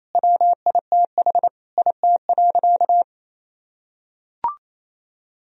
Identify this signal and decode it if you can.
Morse code, heard as a keyed tone, about 28 wpm, 700 Hz. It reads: WITH IT.
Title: Sentences from Top 1000 Words - Review Part 1 - 28wpm